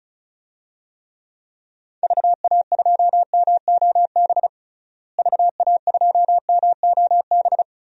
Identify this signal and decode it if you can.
Morse code, heard as a keyed tone, about 36 wpm, 700 Hz. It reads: VA2MOB VA2MOB
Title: DXCC-100 Call Signs - 6 Characters - Repeated 3x - 35wpm